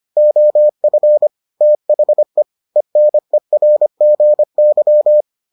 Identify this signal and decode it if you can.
Morse code, heard as a keyed tone, about 25 wpm, 600 Hz. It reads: OF THE ENERGY